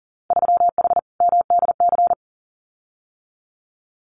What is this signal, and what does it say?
Morse code, heard as a keyed tone, about 38 wpm, 700 Hz. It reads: 3H MDC
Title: Winter Field Day Exchanges Encore 40wpm